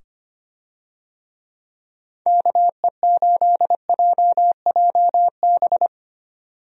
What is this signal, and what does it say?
Morse code, heard as a keyed tone, about 25 wpm, 700 Hz. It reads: KE8JJB
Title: USA Call Signs - 6 Characters - 25wpm